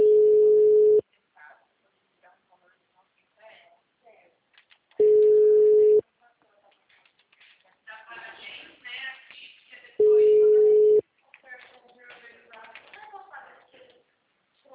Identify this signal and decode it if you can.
Ringback tone (Europe, 425 Hz), with speech.
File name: out- -2008- - - .50149